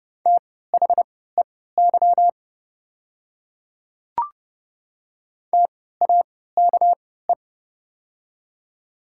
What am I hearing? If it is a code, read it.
Morse code, heard as a keyed tone, about 30 wpm, spaced as for 17 wpm, 690 Hz. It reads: THEY TAKE
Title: 3x Character Spacing - Top 200 Words - 30wpm